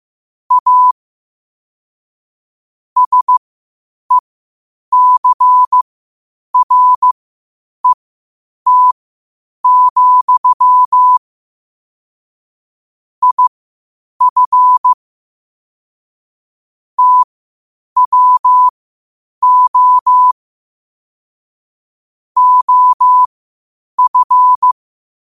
Morse code, heard as a keyed tone, about 15 wpm, spaced as for 8 wpm, 1000 Hz. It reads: A SECRET, IF TWO OF
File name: qotdfiles / 8wpm